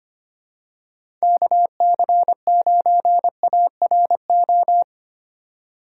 Morse code, heard as a keyed tone, about 25 wpm, 700 Hz. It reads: KC9ARO
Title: USA Call Signs - 6 Characters - 2x Character Spacing - 25wpm